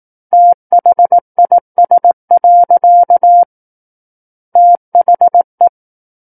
Morse code, heard as a keyed tone, about 18 wpm, 700 Hz. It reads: THIS. THE
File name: 18 WPM